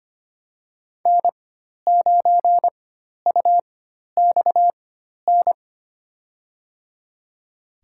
Morse code, heard as a keyed tone, about 25 wpm, spaced as for 12 wpm, 700 Hz. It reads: N9UXN
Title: USA Call Signs - 5 Characters - 4x Character Spacing - 25wpm